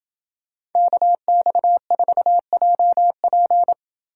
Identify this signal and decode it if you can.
Morse code, heard as a keyed tone, about 28 wpm, 700 Hz. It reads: KX4JP